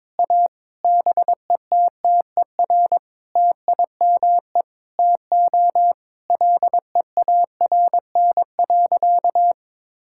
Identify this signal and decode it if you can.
Morse code, heard as a keyed tone, about 22 wpm, 700 Hz. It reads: A BETTER TIME TO LEARN.
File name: practice / 22wpm / Sentences with 8-Word Phrases - 22wpm